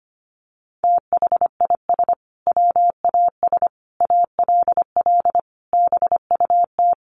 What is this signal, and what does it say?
Morse code, heard as a keyed tone, about 25 wpm, 700 Hz. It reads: THIS WAS ALL BUT
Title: Sentences from Top 1400 Words 25wpm